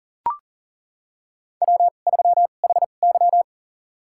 Morse code, heard as a keyed tone, about 40 wpm, spaced as for 29 wpm, 700 Hz. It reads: W3HY
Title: USA Call Signs - 4 Characters - 2x Character Spacing - 40wpm